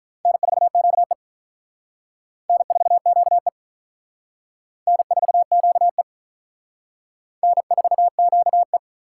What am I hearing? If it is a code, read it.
Morse code, heard as a keyed tone, about 46 wpm, 700 Hz. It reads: N4QE N4QE N4QE N4QE